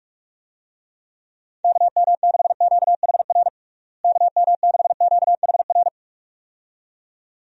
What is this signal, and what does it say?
Morse code, heard as a keyed tone, about 46 wpm, 700 Hz. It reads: KM6QHR KM6QHR
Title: USA Call Signs - 6 Characters - Repeated 2x - 45wpm